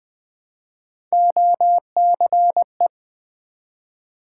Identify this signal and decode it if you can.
Morse code, heard as a keyed tone, about 20 wpm, 700 Hz. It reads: OCE
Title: Build A Word - 1 Letter At A Time - 5-Letter Words - 20wpm